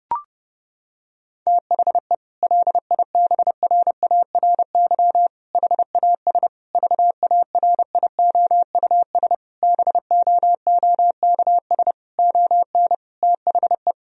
Morse code, heard as a keyed tone, about 32 wpm, 700 Hz. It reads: THE LIBRARY HAS VARIOUS BOOKS ON THE
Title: Sentences from Top 1000 Words - Review Part 4 - 30wpm